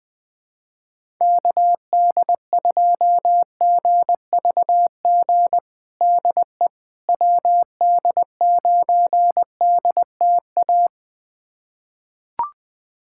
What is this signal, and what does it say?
Morse code, heard as a keyed tone, about 20 wpm, 700 Hz. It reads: KD2GVG DE WD9DTA
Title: USA Callsign DE Callsign - 6 Characters - 20wpm